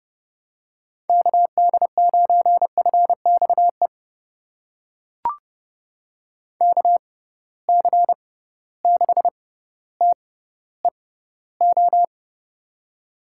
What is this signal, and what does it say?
Morse code, heard as a keyed tone, about 30 wpm, spaced as for 15 wpm, 700 Hz. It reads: KD9FXE KC6TEO
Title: USA Call Signs - 6 Characters - 6x Character Spacing - 30wpm